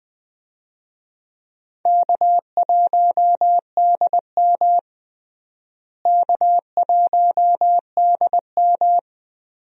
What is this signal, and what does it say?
Morse code, heard as a keyed tone, about 20 wpm, 700 Hz. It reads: K1DM K1DM
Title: USA Call Signs - 4 Characters - Repeated 2x - 20wpm